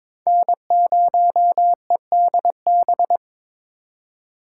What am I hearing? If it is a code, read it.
Morse code, heard as a keyed tone, about 22 wpm, 700 Hz. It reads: N0EDB
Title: USA Call Signs - 5 Characters - 2x Character Spacing - 22wpm